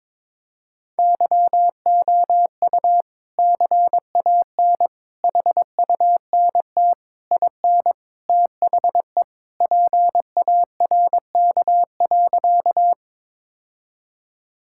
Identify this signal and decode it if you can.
Morse code, heard as a keyed tone, about 22 wpm, 700 Hz. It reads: YOU CAN HUNT IN THE PARK.
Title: Sentences from Top 1600 Words 22wpm